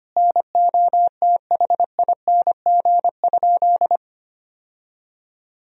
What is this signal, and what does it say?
Morse code, heard as a keyed tone, about 25 wpm, 700 Hz. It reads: NOTHING?
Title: Sentences from Top 1000 Words - Review Part 2 - 25wpm